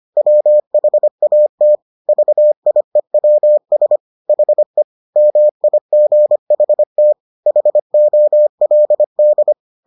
Morse code, heard as a keyed tone, about 25 wpm, 600 Hz. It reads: WHAT VIEWS HE MIGHT HOLD